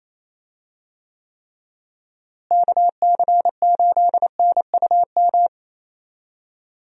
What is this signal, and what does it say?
Morse code, heard as a keyed tone, about 28 wpm, 700 Hz. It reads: KC8NUM
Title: USA Call Signs - 6 Characters - Repeated 3x - 28wpm